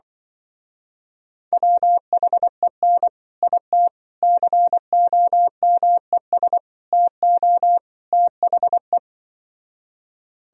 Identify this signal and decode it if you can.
Morse code, heard as a keyed tone, about 24 wpm, 700 Hz. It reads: WHEN IT COMES TO THE
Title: Speed-Racing - Sets of 5 Words 20wpm